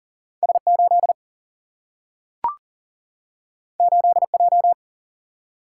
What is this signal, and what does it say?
Morse code, heard as a keyed tone, about 40 wpm, 700 Hz. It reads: S8 8J
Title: Single-Letter Single-Number 40wpm